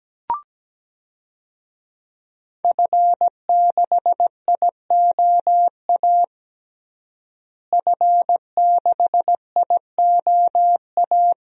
Morse code, heard as a keyed tone, about 17 wpm, 700 Hz. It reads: F6IOA F6IOA
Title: DXCC-100 Call Signs - 5 Characters - Repeated 3x - 17wpm